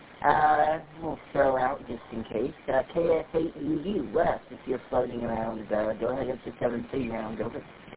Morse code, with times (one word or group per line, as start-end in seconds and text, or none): none